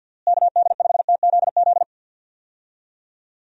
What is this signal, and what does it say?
Morse code, heard as a keyed tone, about 51 wpm, 700 Hz. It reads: KD5TZZ
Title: USA Call Signs - 6 Characters - Repeated 2x - 50wpm